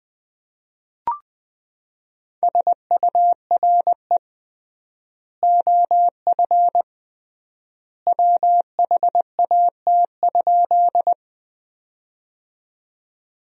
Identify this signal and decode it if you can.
Morse code, heard as a keyed tone, about 21 wpm, 700 Hz. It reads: SURE OF WHAT?
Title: 3x Word-Spacing - Sentences from Top 300 Words - 20wpm